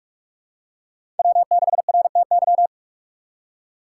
Morse code, heard as a keyed tone, about 44 wpm, 700 Hz. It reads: W6RTY
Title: USA Call Signs - 5 Characters - 2x Character Spacing - 45wpm